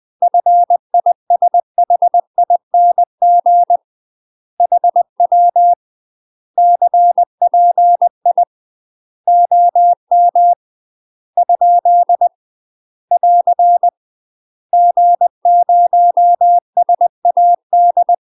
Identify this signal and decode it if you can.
Morse code, heard as a keyed tone, about 20 wpm, 700 Hz. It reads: FISHING HW CPI OM ? <AR> G0SAD
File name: UK MASTER QSOs TRAINING / 20WPM / UK MASTER QSOs 0050